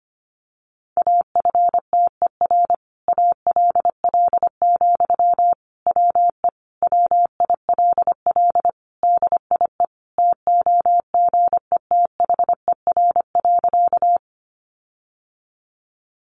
Morse code, heard as a keyed tone, about 25 wpm, 700 Hz. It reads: AFTER ALL, WE WILL DIE TOGETHER.
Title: Sentences from Top 1300 Words 25wpm